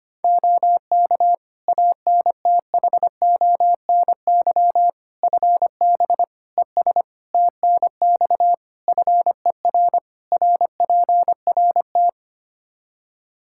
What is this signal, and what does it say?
Morse code, heard as a keyed tone, about 25 wpm, 700 Hz. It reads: OK ANTHONY FB ES TNX FER RPRT
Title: Practice QSO - Exchange 2 - Info - 25wpm